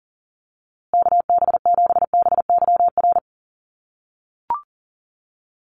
Morse code, heard as a keyed tone, about 40 wpm, 700 Hz. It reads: KB7BYR
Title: USA Call Signs Repeated 2x 40wpm